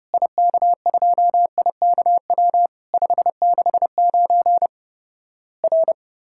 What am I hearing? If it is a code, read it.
Morse code, heard as a keyed tone, about 30 wpm, 700 Hz. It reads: IK2IKW 569 R
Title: Worked All Europe DX Contest - QTC Exchange - 30wpm